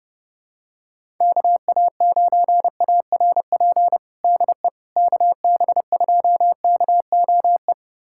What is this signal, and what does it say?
Morse code, heard as a keyed tone, about 30 wpm, 700 Hz. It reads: KA9ARP DE KB2KOE
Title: USA Callsign DE Callsign - 6 Characters - 30wpm